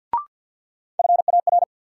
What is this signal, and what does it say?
Morse code, heard as a keyed tone, about 46 wpm, 700 Hz. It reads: FAR
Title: Sets of 3 Words - New Word Drill - 50wpm